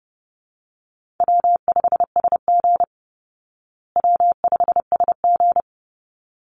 Morse code, heard as a keyed tone, about 30 wpm, 700 Hz. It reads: W5SG W5SG
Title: Easy Call Signs Repeated 2x - 30wpm